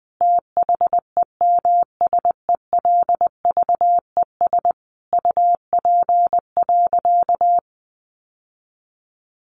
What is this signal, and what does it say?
Morse code, heard as a keyed tone, about 20 wpm, 700 Hz. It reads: THEMSELVES UP.
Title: Sentences from Top 1400 Words 20wpm